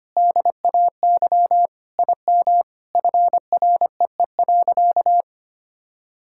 Morse code, heard as a keyed tone, about 25 wpm, 700 Hz. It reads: DAY IM FREE.
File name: practice / 25wpm / Sentences with 3-Word Phrases - 25wpm